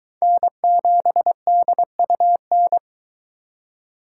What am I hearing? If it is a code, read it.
Morse code, heard as a keyed tone, about 23 wpm, 700 Hz. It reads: N7DUN